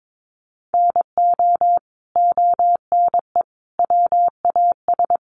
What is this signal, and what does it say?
Morse code, heard as a keyed tone, about 22 wpm, 700 Hz. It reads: NO ONE WAS
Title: Sentences from Top 1200 Words 22wpm